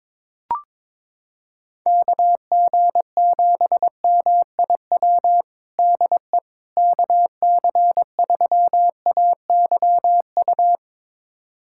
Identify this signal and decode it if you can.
Morse code, heard as a keyed tone, about 22 wpm, 700 Hz. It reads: KG7MIW DE KC3AYU
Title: USA Callsign DE Callsign 22wpm